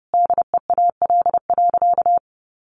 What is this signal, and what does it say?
Morse code, heard as a keyed tone, about 30 wpm, 700 Hz. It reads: DEAL.
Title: Sentences from Top 800 Words 30wpm